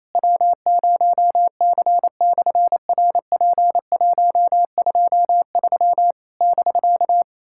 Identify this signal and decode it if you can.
Morse code, heard as a keyed tone, about 27 wpm, 700 Hz. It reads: W0C/RP123 <BK>
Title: SOTA QSO Practice 28wpm